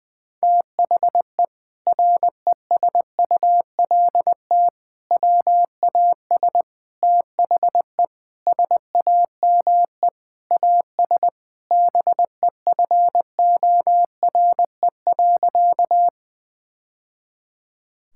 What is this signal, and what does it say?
Morse code, heard as a keyed tone, about 20 wpm, 700 Hz. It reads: THE RESULT WAS THE SAME AS BEFORE.
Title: Sentences from Top 1000 Words - Review Part 5 - 20wpm